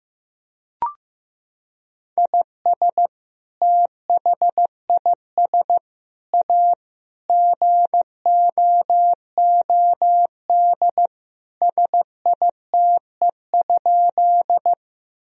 Morse code, heard as a keyed tone, about 14 wpm, 700 Hz. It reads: IS THIS A GOOD SITE?
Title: Sentences from Top 1000 Words - Review Part 4 - 15wpm